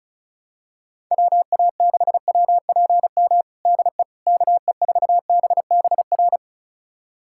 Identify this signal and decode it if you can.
Morse code, heard as a keyed tone, about 36 wpm, 700 Hz. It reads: WA6WPM DE KE4BBR
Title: USA Callsign DE Callsign - 6 Characters - 35wpm